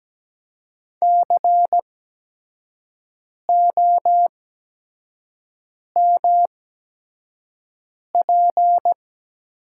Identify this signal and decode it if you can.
Morse code, heard as a keyed tone, about 17 wpm, spaced as for 5 wpm, 700 Hz. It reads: COMP
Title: 8x Character Spacing - Top 500 Words - 17wpm